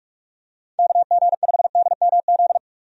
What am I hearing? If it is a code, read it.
Morse code, heard as a keyed tone, about 45 wpm, 700 Hz. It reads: KG5DMZ
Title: USA Call Signs - 6 Characters - 45wpm